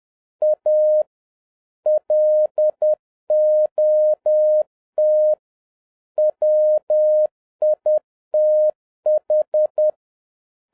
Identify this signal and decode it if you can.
Morse code, heard as a keyed tone, about 10 wpm, 610 Hz. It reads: A LOT WITH